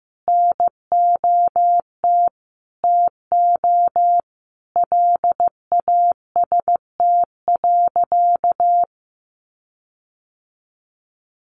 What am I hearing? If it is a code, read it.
Morse code, heard as a keyed tone, about 15 wpm, 700 Hz. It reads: NOT TO LAST.
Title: Sentences from Top 1500 Words 15wpm